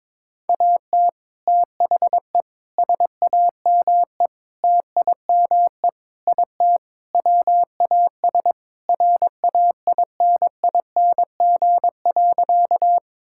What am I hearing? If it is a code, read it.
Morse code, heard as a keyed tone, about 21 wpm, 700 Hz. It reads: AT THE SAME TIME IT WAS RAINING.
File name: practice / 22wpm / Sentences with 3-Word Phrases - 22wpm